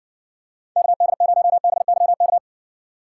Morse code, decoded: KD0BYK

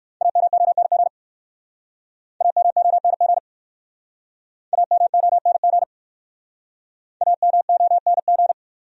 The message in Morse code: AMONG AMONG AMONG AMONG

